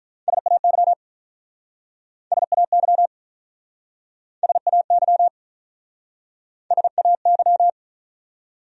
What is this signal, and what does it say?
Morse code, heard as a keyed tone, about 47 wpm, 700 Hz. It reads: SAY SAY SAY SAY